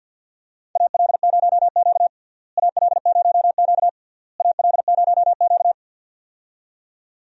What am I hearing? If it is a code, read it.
Morse code, heard as a keyed tone, about 52 wpm, 700 Hz. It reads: AL0Q AL0Q AL0Q
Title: USA Call Signs - 4 Characters - Repeated 3x - 50wpm